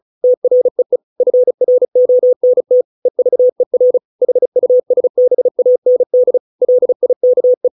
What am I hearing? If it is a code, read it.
Morse code, heard as a keyed tone, about 34 wpm, 500 Hz. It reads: TREE FRONT EVER HUSBAND LIKE